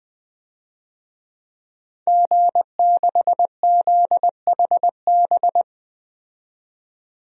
Morse code, heard as a keyed tone, about 20 wpm, 700 Hz. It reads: G6ZHB
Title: DXCC-100 Call Signs - 5 Characters - 20wpm